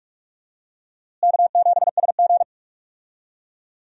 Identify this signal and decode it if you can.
Morse code, heard as a keyed tone, about 43 wpm, 700 Hz. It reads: K7SG
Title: USA Call Signs - 4 Characters - Repeated 3x - 45wpm